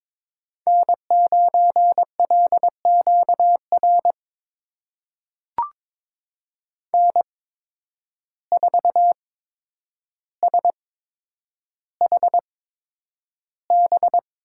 Morse code, heard as a keyed tone, about 22 wpm, spaced as for 9 wpm, 700 Hz. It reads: N9LQR N4SHB
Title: USA Call Signs - 8x Character Spacing - 22wpm